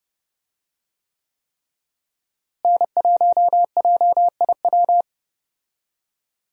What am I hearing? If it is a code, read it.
Morse code, heard as a keyed tone, about 30 wpm, 700 Hz. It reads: N1JIW
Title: USA Call Signs - 5 Characters - Repeated 3x - 30wpm